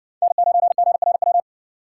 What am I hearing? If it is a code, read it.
Morse code, heard as a keyed tone, about 57 wpm, 700 Hz. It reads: N9GRW